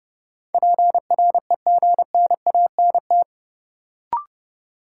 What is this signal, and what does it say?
Morse code, heard as a keyed tone, about 29 wpm, 700 Hz. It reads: PREGNANT
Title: Sentences from Top 1000 Words - Review Part 6 - New Word Drill - 30wpm